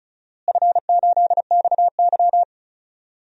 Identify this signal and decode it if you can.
Morse code, heard as a keyed tone, about 36 wpm, 700 Hz. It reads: F8XY
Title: DXCC-100 Call Signs - 4 Characters - Repeated 2x - 35wpm